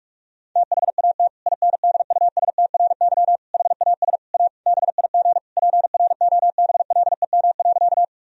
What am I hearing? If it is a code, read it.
Morse code, heard as a keyed tone, about 46 wpm, 700 Hz. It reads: THAT INDUSTRY HAS A BIG PROBLEM.
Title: Sentences from Top 1000 Words - Review Part 4 - 45wpm